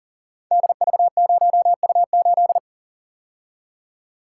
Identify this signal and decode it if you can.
Morse code, heard as a keyed tone, about 44 wpm, 700 Hz. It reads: DV0U8
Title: Code Groups - 5 Letter:Numbers - 40wpm